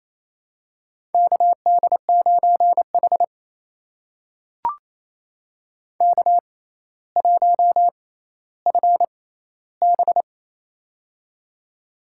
Morse code, heard as a keyed tone, about 28 wpm, spaced as for 14 wpm, 700 Hz. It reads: KD9H K1FB
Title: USA Call Signs - 4 Characters - 6x Character Spacing - 28wpm